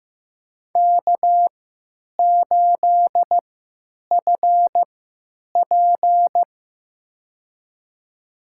Morse code, heard as a keyed tone, about 15 wpm, spaced as for 9 wpm, 700 Hz. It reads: K8FP
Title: USA Call Signs - 3x Character Spacing - 15wpm